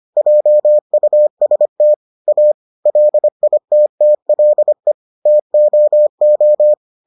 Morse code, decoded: JUST A LITTLE TOO

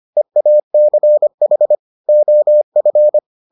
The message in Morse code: EACH OF

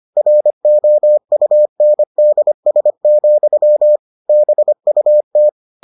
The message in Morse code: ROUNDS, BUT

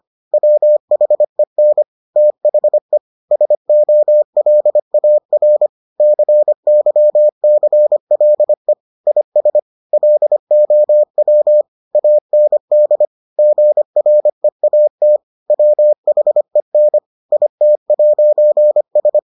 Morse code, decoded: WHEN THE SOLAR CYCLE IS LOW AND GREAT WHEN IT'S

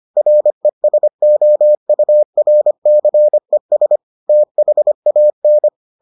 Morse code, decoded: RESOURCES THAN

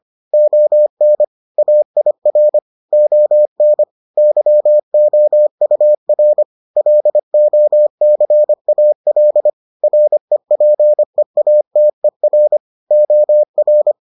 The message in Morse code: ON AIR ON YOUR LOCAL REPEATER OR

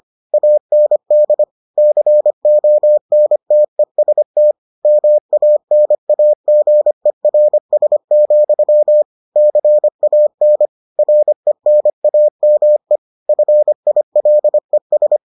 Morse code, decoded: AND CONTEST MANAGERS, CAN RENAME FILES

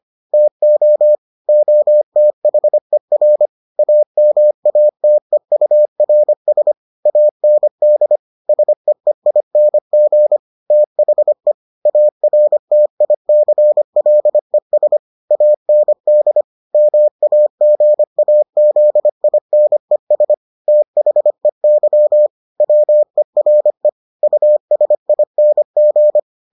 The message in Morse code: TO OTHER AMATEURS AND SEEING THE ARTICLES AND MAGAZINES THEY WERE USING

